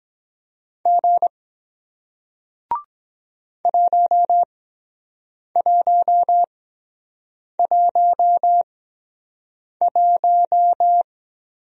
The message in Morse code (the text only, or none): G 1111